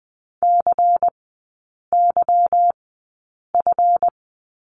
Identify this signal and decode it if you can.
Morse code, heard as a keyed tone, about 20 wpm, spaced as for 8 wpm, 690 Hz. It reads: CYF
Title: 145 Lesson 24e - Letter y - Three Characters 20wpm